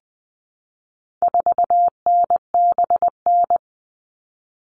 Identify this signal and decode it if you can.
Morse code, heard as a keyed tone, about 20 wpm, 700 Hz. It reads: 4NBN
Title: All Cut Numbers - 4-Digit 20wpm